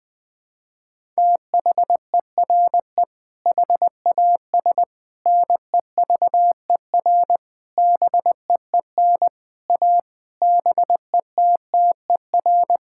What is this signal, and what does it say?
Morse code, heard as a keyed tone, about 21 wpm, 700 Hz. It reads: THERE HAS NEVER BEEN A BETTER
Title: Sentences with 8-Word Phrases - 20wpm